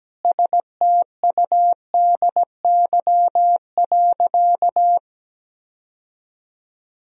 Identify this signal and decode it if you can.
Morse code, heard as a keyed tone, about 17 wpm, 700 Hz. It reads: STUDY.
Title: Sentences from Top 1000 Words - Review Part 1 - 17wpm